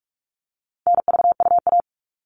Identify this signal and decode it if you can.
Morse code, heard as a keyed tone, about 44 wpm, 700 Hz. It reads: NVUA
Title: All Cut Numbers - 4-Digit 45wpm